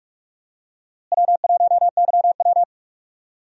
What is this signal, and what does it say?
Morse code, heard as a keyed tone, about 43 wpm, 700 Hz. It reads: W1YW
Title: USA Call Signs - 4 Characters - Repeated 2x - 45wpm